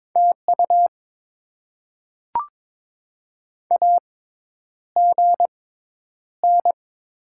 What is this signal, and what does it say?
Morse code, heard as a keyed tone, about 22 wpm, spaced as for 10 wpm, 700 Hz. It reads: TU AGN
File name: Top 20 QSO Elements - 6x Character Spacing - 22wpm